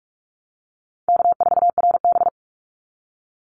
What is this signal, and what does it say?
Morse code, heard as a keyed tone, about 44 wpm, 700 Hz. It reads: K4RB
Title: Easy Call Signs - 45wpm